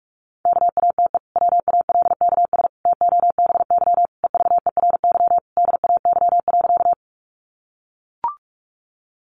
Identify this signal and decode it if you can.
Morse code, heard as a keyed tone, about 44 wpm, 700 Hz. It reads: KATE WALKS TOBY EVERY DAY.
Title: Easy Phrases 45wpm